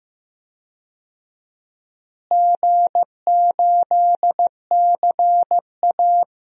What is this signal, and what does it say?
Morse code, heard as a keyed tone, about 15 wpm, 700 Hz. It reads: G8CA
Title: DXCC-100 Call Signs - 4 Characters - Repeated 2x - 15wpm